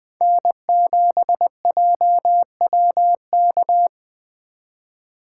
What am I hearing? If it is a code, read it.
Morse code, heard as a keyed tone, about 21 wpm, 700 Hz. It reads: N7JWK